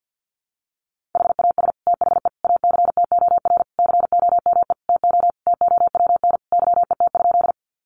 Morse code, heard as a keyed tone, about 50 wpm, 700 Hz. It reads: HAS THE ACTOR COME TO TOWN YET?
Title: Sentences from Top 1500 Words 50wpm